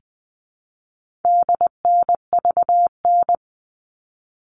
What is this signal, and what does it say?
Morse code, heard as a keyed tone, about 20 wpm, 700 Hz. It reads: DNVN
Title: All Cut Numbers - 4-Digit 20wpm